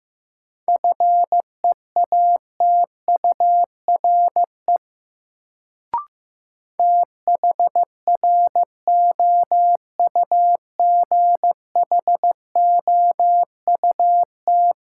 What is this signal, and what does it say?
Morse code, heard as a keyed tone, about 15 wpm, 700 Hz. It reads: FEATURE THROUGHOUT
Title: Sentences from Top 800 Words - New Word Drill - 15wpm